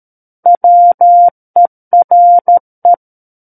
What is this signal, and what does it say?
Morse code, heard as a keyed tone, about 13 wpm, 700 Hz. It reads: WERE
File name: Book One / 04-Chapter 4 - 13 WPM